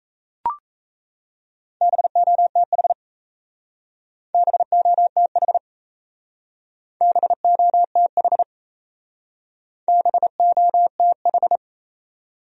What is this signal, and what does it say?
Morse code, heard as a keyed tone, about 36 wpm, 700 Hz. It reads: BOTH BOTH BOTH BOTH